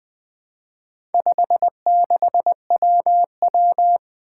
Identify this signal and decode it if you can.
Morse code, heard as a keyed tone, about 20 wpm, 700 Hz. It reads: 56WW
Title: Four Challenging Letter-Number Combinations 20wpm